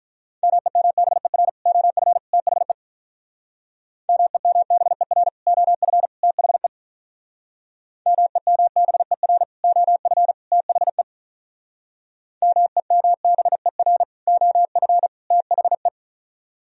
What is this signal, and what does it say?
Morse code, heard as a keyed tone, about 45 wpm, 700 Hz. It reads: MEMBER OF THE MEMBER OF THE MEMBER OF THE MEMBER OF THE